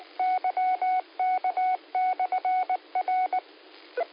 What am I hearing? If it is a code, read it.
Morse code, heard as a keyed tone, about 19 wpm, 740 Hz. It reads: YK/R
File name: qsorec SimplexLogic